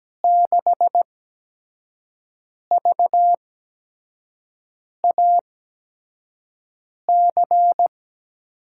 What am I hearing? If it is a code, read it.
Morse code, heard as a keyed tone, about 17 wpm, 700 Hz. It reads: 6 V A C
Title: USA Call Signs - 5 Characters - 8x Character Spacing - 17wpm